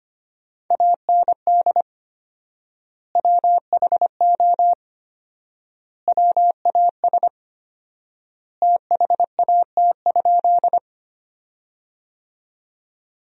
Morse code, heard as a keyed tone, about 25 wpm, 700 Hz. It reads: AND WHO WAS THAT?
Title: 4x Word-Spacing - Sentences from Top 100 Words - 25wpm